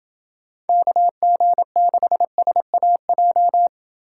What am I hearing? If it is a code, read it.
Morse code, heard as a keyed tone, about 28 wpm, 700 Hz. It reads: KG6SAJ